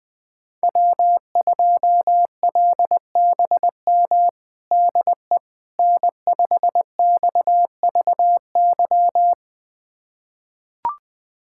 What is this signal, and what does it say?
Morse code, heard as a keyed tone, about 20 wpm, 700 Hz. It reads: W2LBM DE N5XVY
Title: USA Callsign DE Callsign 20wpm